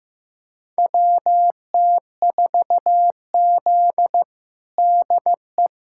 Morse code, heard as a keyed tone, about 15 wpm, 700 Hz. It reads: WT4Z DE
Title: USA Callsign DE Callsign - 4 Characters - 15wpm